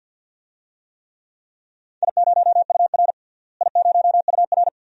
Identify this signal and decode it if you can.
Morse code, heard as a keyed tone, about 47 wpm, 700 Hz. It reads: I0UR I0UR
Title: DXCC-100 Call Signs - 4 Characters - Repeated 2x - 50wpm